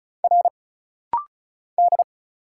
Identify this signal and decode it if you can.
Morse code, heard as a keyed tone, about 35 wpm, 700 Hz. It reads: R D
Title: Single Letter-Number Alt - Rapid-Fire 35wpm